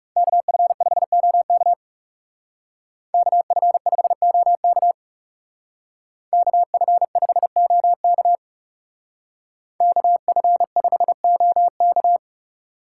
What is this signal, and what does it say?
Morse code, heard as a keyed tone, about 37 wpm, 700 Hz. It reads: KF5OK KF5OK KF5OK KF5OK
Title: Speed Racing - USA 2x2 Call Signs 30wpm